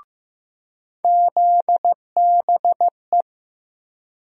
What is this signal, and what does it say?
Morse code, heard as a keyed tone, about 15 wpm, 700 Hz. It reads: ZBE
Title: USA Call Sign Postfixes - 3 Letters 15wpm